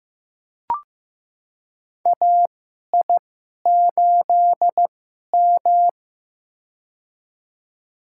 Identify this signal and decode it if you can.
Morse code, heard as a keyed tone, about 15 wpm, spaced as for 11 wpm, 700 Hz. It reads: AI8M
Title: USA Call Signs - 4 Characters - 2x Character Spacing - 15wpm